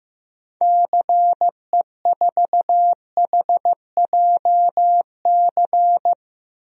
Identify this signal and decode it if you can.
Morse code, heard as a keyed tone, about 15 wpm, 700 Hz. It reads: CE4HJC
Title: DXCC-100 Call Signs - 6 Characters - Repeated 2x - 15wpm